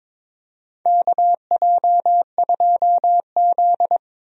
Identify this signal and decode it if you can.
Morse code, heard as a keyed tone, about 22 wpm, 700 Hz. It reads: KJ2Z
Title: USA Call Signs - 4 Characters - Repeated 2x - 22wpm